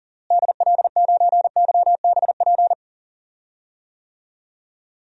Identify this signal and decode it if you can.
Morse code, heard as a keyed tone, about 40 wpm, 700 Hz. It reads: DL9YBP
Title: DXCC-100 Call Signs - 6 Characters - Repeated 2x - 40wpm